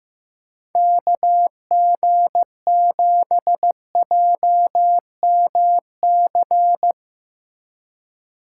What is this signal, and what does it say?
Morse code, heard as a keyed tone, about 16 wpm, 700 Hz. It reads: KG7JMC
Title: Speed Racing - USA 2x3 Call Signs 15wpm